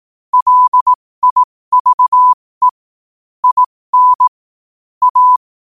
Morse code, heard as a keyed tone, about 18 wpm, spaced as for 15 wpm, 1000 Hz. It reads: LIVE IN A
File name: qotdfiles / 15wpm